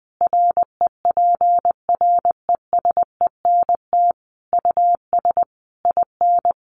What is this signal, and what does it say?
Morse code, heard as a keyed tone, about 19 wpm, 700 Hz. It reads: REPRESENT US IN
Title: Sentences from Top 1200 Words 20wpm